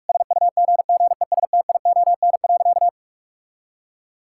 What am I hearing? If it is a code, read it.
Morse code, heard as a keyed tone, about 46 wpm, 700 Hz. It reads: SUGGESTION.
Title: Sentences from Top 1600 Words 45wpm